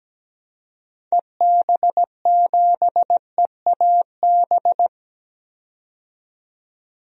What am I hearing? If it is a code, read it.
Morse code, heard as a keyed tone, about 17 wpm, 700 Hz. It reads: EB7EAB
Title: DXCC-100 Call Signs - 6 Characters - 17wpm